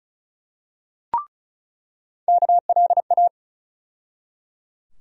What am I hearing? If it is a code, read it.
Morse code, heard as a keyed tone, about 34 wpm, 700 Hz. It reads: KLA
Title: USA Call Sign Postfixes - 3 Letters 35wpm